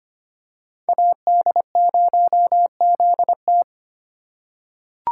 Morse code, heard as a keyed tone, about 25 wpm, 700 Hz. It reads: AD0ZT
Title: USA Call Signs - 4x Character Spacing - 25wpm